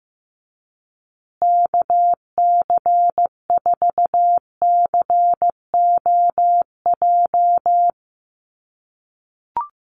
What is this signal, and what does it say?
Morse code, heard as a keyed tone, about 15 wpm, 700 Hz. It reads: KC4COJ
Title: USA Call Signs Repeated 2x 15wpm